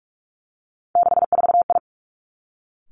Morse code, heard as a keyed tone, about 46 wpm, 700 Hz. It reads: 64I
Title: Three Challenging Letter-Number Combinations 45wpm